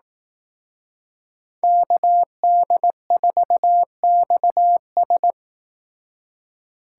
Morse code, heard as a keyed tone, about 18 wpm, 700 Hz. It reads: KD4XS
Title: Speed Racing - USA 2x2 Call Signs 15wpm